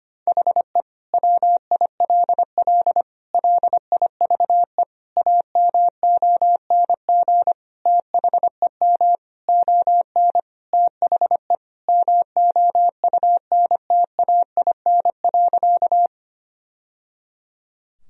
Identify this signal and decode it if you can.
Morse code, heard as a keyed tone, about 25 wpm, 700 Hz. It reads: HE WILL LIVE AMONG THEM ON THE MOUNTAIN.
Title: Sentences from Top 1000 Words - Review Part 3 - 25wpm